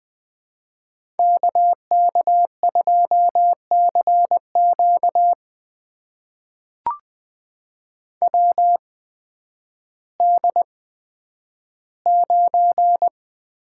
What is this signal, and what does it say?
Morse code, heard as a keyed tone, about 20 wpm, spaced as for 11 wpm, 700 Hz. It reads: KK2CQ WD9
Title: USA Call Signs - 8x Character Spacing - 20wpm